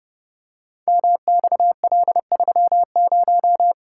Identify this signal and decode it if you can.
Morse code, heard as a keyed tone, about 30 wpm, 700 Hz. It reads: MXL30
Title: Code Groups - 5 Letter:Numbers - 30wpm